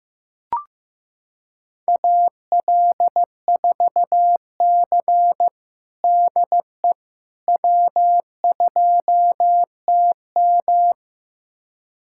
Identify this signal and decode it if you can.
Morse code, heard as a keyed tone, about 15 wpm, 700 Hz. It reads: AL4C DE W2TM
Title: USA Callsign DE Callsign - 4 Characters - 15wpm